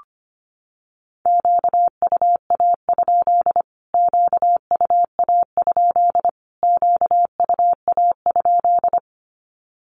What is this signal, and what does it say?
Morse code, heard as a keyed tone, about 25 wpm, 700 Hz. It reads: QUA? QUA? QUA?
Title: Q-Codes Repeated 3x 25wpm